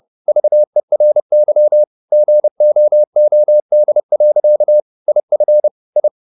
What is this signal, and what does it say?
Morse code, heard as a keyed tone, about 30 wpm, 600 Hz. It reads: VERY GOOD. IF I